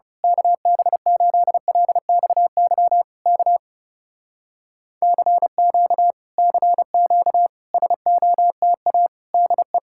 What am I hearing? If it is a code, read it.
Morse code, heard as a keyed tone, about 28 wpm, 700 Hz. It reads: KB8LXY K CQ CQ SOTA DE
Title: Speed-Racing Common QSO Phrases - 30wpm